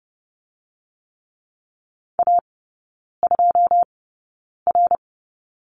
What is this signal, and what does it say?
Morse code, heard as a keyed tone, about 30 wpm, spaced as for 9 wpm, 700 Hz. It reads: A2R
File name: Lesson 16e - Letter c - Three Characters 30wpm